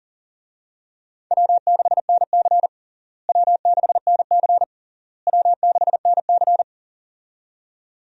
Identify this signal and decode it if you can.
Morse code, heard as a keyed tone, about 40 wpm, 700 Hz. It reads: W6NC W6NC W6NC
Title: USA Call Signs - 4 Characters - Repeated 3x - 40wpm